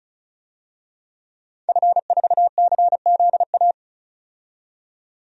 Morse code, heard as a keyed tone, about 35 wpm, 700 Hz. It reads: F4CZA
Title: DXCC-100 Call Signs - 5 Characters - 35wpm